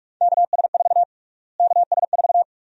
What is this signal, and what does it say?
Morse code, heard as a keyed tone, about 42 wpm, 700 Hz. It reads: KS4 KS4
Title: USA Call Sign Prefixes - Repeated 2x - 45wpm